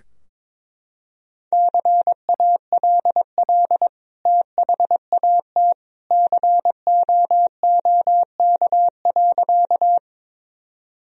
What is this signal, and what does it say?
Morse code, heard as a keyed tone, about 22 wpm, 700 Hz. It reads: CALL THAT COOK.
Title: Sentences from Top 1000 Words - Review Part 4 - 22wpm